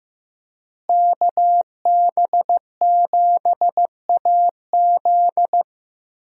Morse code, heard as a keyed tone, about 15 wpm, 700 Hz. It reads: KB7AZ
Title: USA Call Signs - 5 Characters - Repeated 3x - 15wpm